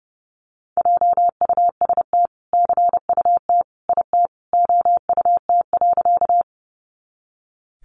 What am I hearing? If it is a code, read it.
Morse code, heard as a keyed tone, about 30 wpm, 700 Hz. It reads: JUST CUT IT OUT.
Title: Sentences from Top 300 Words 30wpm